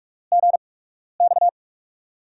Morse code, heard as a keyed tone, about 44 wpm, 700 Hz. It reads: G X
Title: USA Call Signs - 4 Characters - 8x Character Spacing - 45wpm